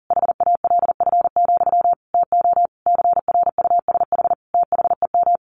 Morse code, heard as a keyed tone, about 40 wpm, 700 Hz. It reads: HALF, TO CRUSH THEM
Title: A Sound Of Thunder 40wpm